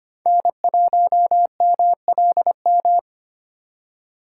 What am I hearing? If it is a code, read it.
Morse code, heard as a keyed tone, about 25 wpm, 700 Hz. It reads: N1MLM
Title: USA Call Signs - 2x Character Spacing - 25wpm